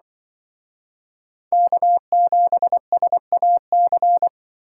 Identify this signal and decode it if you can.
Morse code, heard as a keyed tone, about 24 wpm, 700 Hz. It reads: K7SAC
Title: Speed Racing - USA 1x3 Call Signs 20wpm